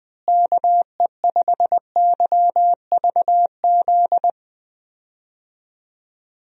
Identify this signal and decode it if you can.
Morse code, heard as a keyed tone, about 20 wpm, 700 Hz. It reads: KE5YVZ
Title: USA Call Signs - 6 Characters - Repeated 3x - 20wpm